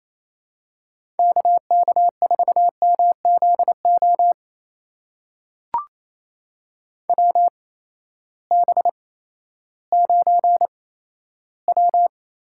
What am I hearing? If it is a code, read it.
Morse code, heard as a keyed tone, about 28 wpm, spaced as for 14 wpm, 700 Hz. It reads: KK4MZO WB9W
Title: USA Call Signs - 6 Characters - 8x Character Spacing - 28wpm